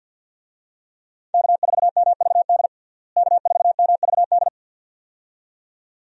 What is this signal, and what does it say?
Morse code, heard as a keyed tone, about 51 wpm, 700 Hz. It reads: K4MVD K4MVD
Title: USA Call Signs - 5 Characters - Repeated 2x - 50wpm